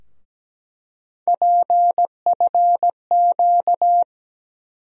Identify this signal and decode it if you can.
Morse code, heard as a keyed tone, about 17 wpm, 700 Hz. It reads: PFQ
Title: USA Call Sign Postfixes - 3 Letters 17wpm